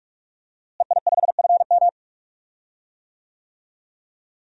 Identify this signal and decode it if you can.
Morse code, heard as a keyed tone, about 44 wpm, 700 Hz. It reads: EI5FM
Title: DXCC-100 Call Signs - 5 Characters - Repeated 2x - 45wpm